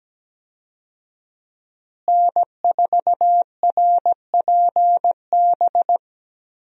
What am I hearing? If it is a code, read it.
Morse code, heard as a keyed tone, about 17 wpm, 700 Hz. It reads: N4RPB
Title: USA Call Signs - 5 Characters - Repeated 3x - 17wpm